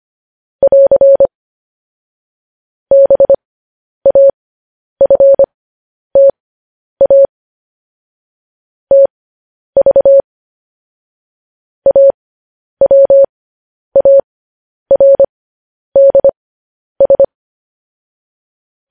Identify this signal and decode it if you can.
Morse code, heard as a keyed tone, about 25 wpm, spaced as for 10 wpm, 550 Hz. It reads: <AR> BAFTA TV AWARDS
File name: podcast / News headlines-1x-25x10